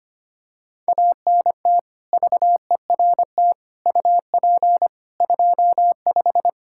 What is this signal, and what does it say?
Morse code, heard as a keyed tone, about 25 wpm, 700 Hz. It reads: ANT VERT UP 25
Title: Practice QSO - Exchange 2 - Info - 25wpm